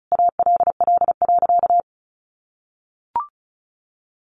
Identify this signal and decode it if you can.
Morse code, heard as a keyed tone, about 34 wpm, 700 Hz. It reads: ALL.
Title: Sentences from Top 500 Words 35wpm